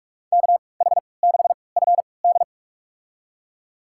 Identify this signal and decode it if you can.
Morse code, heard as a keyed tone, about 46 wpm, spaced as for 26 wpm, 700 Hz. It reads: KH6FD
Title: USA Call Signs - 5 Characters - 3x Character Spacing - 45wpm